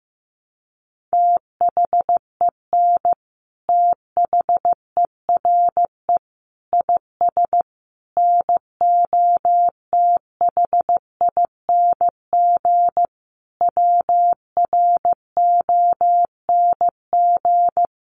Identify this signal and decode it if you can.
Morse code, heard as a keyed tone, about 15 wpm, 700 Hz. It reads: THEN THERE IS NOTHING WRONG
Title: Sentences from Top 900 Words 15wpm